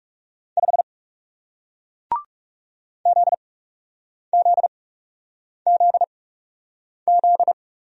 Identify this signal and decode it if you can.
Morse code, heard as a keyed tone, about 38 wpm, spaced as for 8 wpm, 700 Hz. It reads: 5 ZZZZ